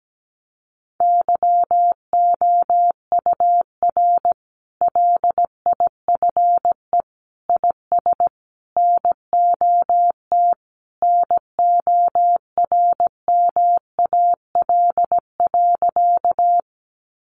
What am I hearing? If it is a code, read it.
Morse code, heard as a keyed tone, about 17 wpm, 700 Hz. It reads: YOUR LIFE IS NOT NORMAL.
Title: Sentences from Top 1200 Words 17wpm